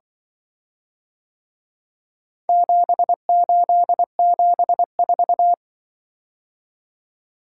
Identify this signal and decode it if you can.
Morse code, heard as a keyed tone, about 24 wpm, 700 Hz. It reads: 7874
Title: Speed-Racing - Four Letter-Number Combinations - 20wpm